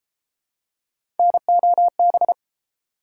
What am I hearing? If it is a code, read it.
Morse code, heard as a keyed tone, about 31 wpm, 700 Hz. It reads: NOB